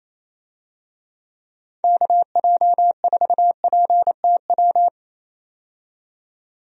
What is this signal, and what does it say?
Morse code, heard as a keyed tone, about 28 wpm, 700 Hz. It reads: KJ4PTW
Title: USA Call Signs - 6 Characters - 28wpm